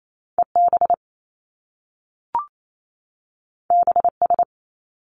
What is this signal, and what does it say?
Morse code, heard as a keyed tone, about 28 wpm, 700 Hz. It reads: EB BS
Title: Top Two Letter Combinations in Words 28wpm